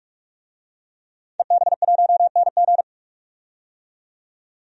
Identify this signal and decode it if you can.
Morse code, heard as a keyed tone, about 44 wpm, 700 Hz. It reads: EB1NG
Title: DXCC-100 Call Signs - 5 Characters - 45wpm